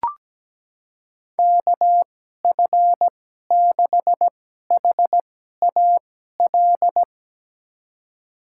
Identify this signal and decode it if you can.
Morse code, heard as a keyed tone, about 17 wpm, spaced as for 12 wpm, 700 Hz. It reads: KF6HAL
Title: USA Call Signs - 6 Characters - 2x Character Spacing - 17wpm